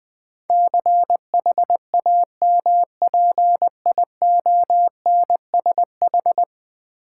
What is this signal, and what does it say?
Morse code, heard as a keyed tone, about 20 wpm, 700 Hz. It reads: CHAMPIONSH
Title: Build A Word - 5 Letters At A Time - 12-Letter Words - 20wpm